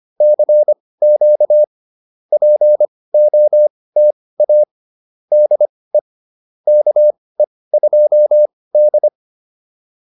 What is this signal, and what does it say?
Morse code, heard as a keyed tone, about 25 wpm, spaced as for 18 wpm, 600 Hz. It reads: CQ POTA DE KE2D
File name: POTA106 18